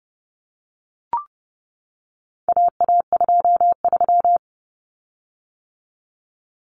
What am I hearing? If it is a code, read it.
Morse code, heard as a keyed tone, about 30 wpm, 700 Hz. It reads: AA23
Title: Common Cut Numbers - 4-Digit 30wpm